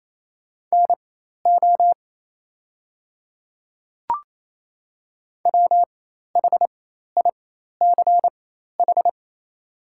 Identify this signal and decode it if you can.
Morse code, heard as a keyed tone, about 27 wpm, spaced as for 13 wpm, 700 Hz. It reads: NO WHICH
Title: 4x Character Spacing - Top 100 Words - 28wpm